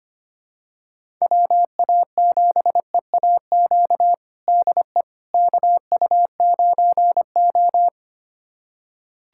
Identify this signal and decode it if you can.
Morse code, heard as a keyed tone, about 25 wpm, 700 Hz. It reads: WA7EAQ DE KU9O
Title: USA Callsign DE Callsign 25wpm